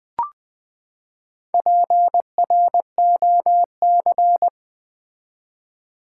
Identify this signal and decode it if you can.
Morse code, heard as a keyed tone, about 20 wpm, 700 Hz. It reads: PROC
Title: Build A Word - 4 Letters At A Time - 10-Letter Words - 20wpm